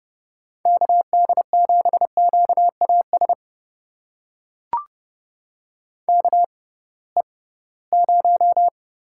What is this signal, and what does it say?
Morse code, heard as a keyed tone, about 31 wpm, spaced as for 20 wpm, 700 Hz. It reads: KD7QAS KE0
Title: USA Call Signs - 6 Characters - 6x Character Spacing - 30wpm